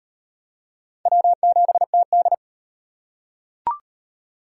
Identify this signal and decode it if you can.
Morse code, heard as a keyed tone, about 36 wpm, 700 Hz. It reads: W7TD